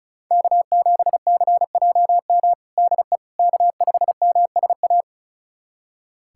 Morse code, heard as a keyed tone, about 36 wpm, 700 Hz. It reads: K7CJM DE K5MSA
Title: USA Callsign DE Callsign - 5 Characters - 35wpm